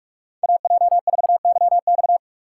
Morse code, heard as a keyed tone, about 59 wpm, spaced as for 52 wpm, 700 Hz. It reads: AJ4YX
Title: Speed Racing - USA 2x2 Call Signs 45wpm